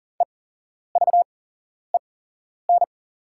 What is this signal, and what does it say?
Morse code, heard as a keyed tone, about 41 wpm, spaced as for 11 wpm, 700 Hz. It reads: EVEN
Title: 8x Character Spacing - Top 300 Words - 40wpm